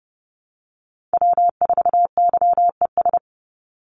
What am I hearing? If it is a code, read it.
Morse code, heard as a keyed tone, about 30 wpm, 700 Hz. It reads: W4YES
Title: Call Signs - 30wpm